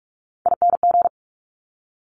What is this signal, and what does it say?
Morse code, heard as a keyed tone, about 46 wpm, 700 Hz. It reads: ING
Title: Top Three Letter Combinations in Words 45wpm